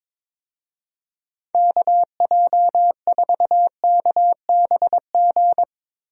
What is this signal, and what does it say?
Morse code, heard as a keyed tone, about 21 wpm, 700 Hz. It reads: KJ4KBG
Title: USA Call Signs - 6 Characters - Repeated 2x - 22wpm